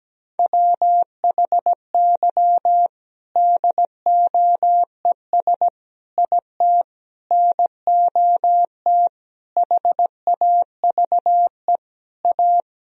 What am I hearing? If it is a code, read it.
Morse code, heard as a keyed tone, about 17 wpm, 700 Hz. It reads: WHY DOES IT NOT HAVE A
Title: Sentences from Top 1000 Words - Review Part 3 - 17wpm